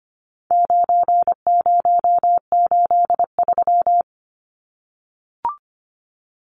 Morse code, heard as a keyed tone, about 25 wpm, 700 Hz. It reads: 9083
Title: Numbers - 4-Digit 25wpm